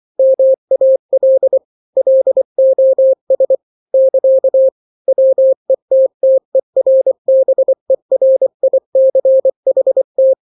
Morse code, heard as a keyed tone, about 24 wpm, 530 Hz. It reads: MAL LOS <KA> WETTERBERICHT
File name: daily / 23wpm